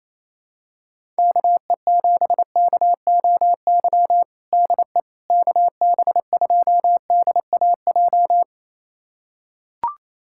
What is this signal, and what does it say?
Morse code, heard as a keyed tone, about 28 wpm, 700 Hz. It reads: KE7KOY DE KB2DAJ
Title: USA Callsign DE Callsign - 6 Characters - 28wpm